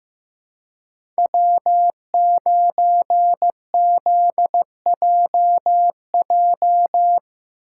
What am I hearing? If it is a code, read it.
Morse code, heard as a keyed tone, about 15 wpm, 700 Hz. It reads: W9ZJJ
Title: USA Call Signs - 5 Characters - 3x Character Spacing - 15wpm